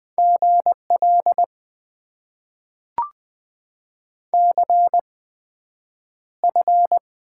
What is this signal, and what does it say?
Morse code, heard as a keyed tone, about 20 wpm, spaced as for 9 wpm, 700 Hz. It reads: GL CF
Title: Top 90 QSO Elements - 8x Character Spacing - 20wpm